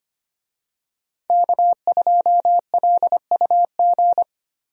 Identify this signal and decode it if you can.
Morse code, heard as a keyed tone, about 25 wpm, 700 Hz. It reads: K2LUG
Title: USA Call Signs - 5 Characters - Repeated 3x - 25wpm